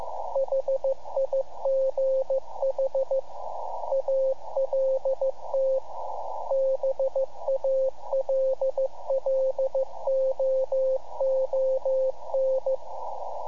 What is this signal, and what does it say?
Morse code, heard as a keyed tone, about 15 wpm, 530 Hz. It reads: HIGH ALT BALLOON
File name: iHAB-3 / Repeater Recordings